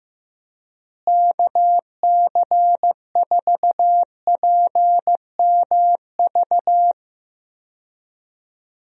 Morse code, heard as a keyed tone, about 15 wpm, 700 Hz. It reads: KC4PMV
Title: USA Call Signs - 6 Characters - Repeated 3x - 15wpm